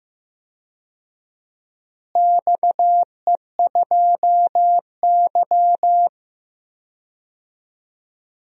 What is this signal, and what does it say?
Morse code, heard as a keyed tone, about 15 wpm, 700 Hz. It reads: XE2Y
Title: DXCC-100 Call Signs - 4 Characters - Repeated 3x - 15wpm